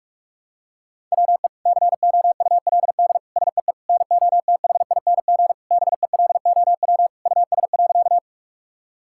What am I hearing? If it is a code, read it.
Morse code, heard as a keyed tone, about 41 wpm, 700 Hz. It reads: WE COULD SEE NOTHING BELOW US.
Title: Sentences from Top 1000 Words - Review Part 5 - 45wpm